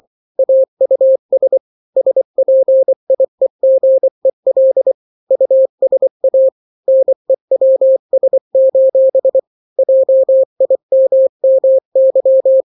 Morse code, heard as a keyed tone, about 24 wpm, 530 Hz. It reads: AUS SPIEGEL USA NEWS: JIMMY
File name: daily / 23wpm